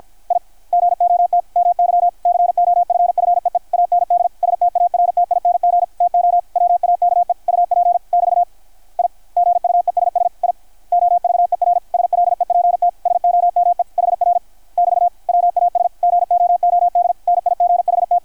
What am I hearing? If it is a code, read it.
Morse code, heard as a keyed tone, about 53 wpm, 660 Hz, with radio noise. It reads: I GOT MY COFFEE AND STARTING TO WAKE UP <BT> I GUESS I OVER SLEPT SOME HR <BT> WAS GOOD NIGHT
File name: archived / CRSnet